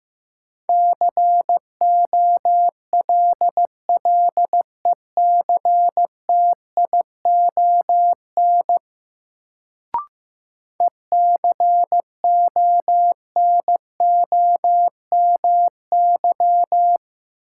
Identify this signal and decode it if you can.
Morse code, heard as a keyed tone, about 15 wpm, 700 Hz. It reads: COLLECTION ECONOMY
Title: Sentences from Top 1200 Words - New Word Drill Alt - 15wpm